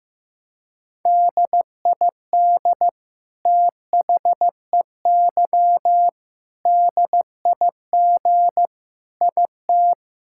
Morse code, heard as a keyed tone, about 15 wpm, 700 Hz. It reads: DID THEY DIG IT
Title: Sentences from Top 1600 Words 15wpm